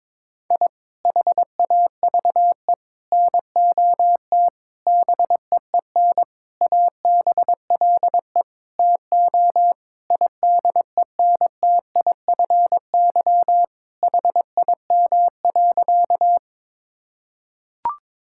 Morse code, heard as a keyed tone, about 21 wpm, 700 Hz. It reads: I HAVE NOT BEEN ABLE TO IDENTIFY HIM.
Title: Sentences from Top 1000 Words - Review Part 6 - 22wpm